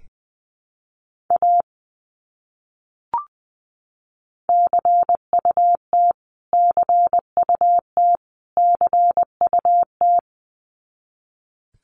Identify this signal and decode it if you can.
Morse code, heard as a keyed tone, about 20 wpm, 700 Hz. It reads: A CUT CUT CUT
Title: Top 300 Words Repeated 20wpm